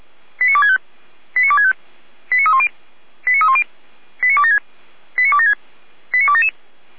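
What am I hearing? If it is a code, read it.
Five-tone selective call (ZVEI), heard as quick runs of five tones, tones of about 70 ms. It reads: 89265 89265 89310 89310 89276 89276 89277
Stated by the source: ZVEI five-tone call received by SDR, from a server of feuerwehr-orlamuende.org (fire brigade Orlamünde).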